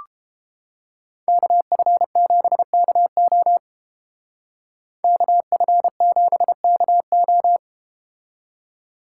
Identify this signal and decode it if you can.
Morse code, heard as a keyed tone, about 32 wpm, 700 Hz. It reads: KF7KO KF7KO